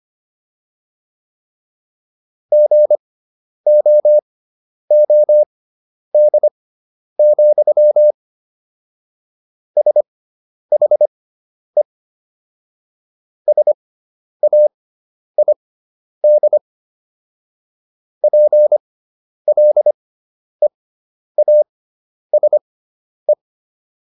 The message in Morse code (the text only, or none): GOOD, SHE SAID PLEASE